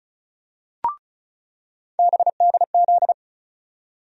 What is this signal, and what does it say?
Morse code, heard as a keyed tone, about 36 wpm, 700 Hz. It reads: BDZ
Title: USA Call Sign Postfixes - 3 Letters 35wpm